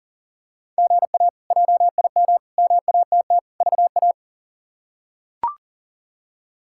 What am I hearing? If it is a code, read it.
Morse code, heard as a keyed tone, about 37 wpm, 700 Hz. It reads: GA JIM MATT VA
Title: K1USN SST QSO Practice - 40wpm